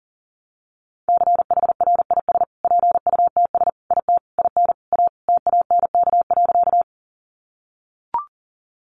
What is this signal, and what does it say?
Morse code, heard as a keyed tone, about 40 wpm, 700 Hz. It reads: CHRIS PUTS IT IN A TANK.
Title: Additional Easy Phrases 40wpm